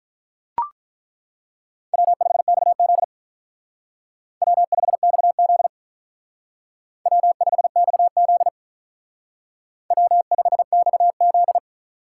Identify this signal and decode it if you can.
Morse code, heard as a keyed tone, about 42 wpm, 700 Hz. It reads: W5XZ W5XZ W5XZ W5XZ
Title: Speed Racing - USA 1x2 Call Signs 35wpm